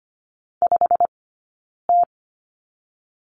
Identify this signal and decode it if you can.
Morse code, heard as a keyed tone, about 25 wpm, spaced as for 9 wpm, 690 Hz. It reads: 5T
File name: Lesson 15d - Number 5 - Two Characters 25wpm